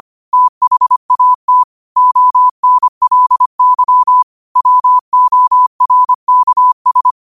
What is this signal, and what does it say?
Morse code, heard as a keyed tone, about 25 wpm, 1000 Hz. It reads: THAT ONLY WORKS
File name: qotdfiles / 25wpm